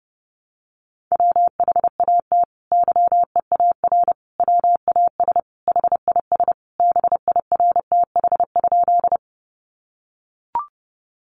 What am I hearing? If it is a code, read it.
Morse code, heard as a keyed tone, about 28 wpm, 700 Hz. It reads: WHAT YEAR WAS HIS BIRTH?
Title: Sentences from Top 800 Words 30wpm